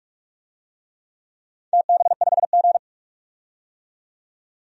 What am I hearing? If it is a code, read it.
Morse code, heard as a keyed tone, about 42 wpm, 700 Hz. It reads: TB5G
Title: DXCC-100 Call Signs - 4 Characters - 45wpm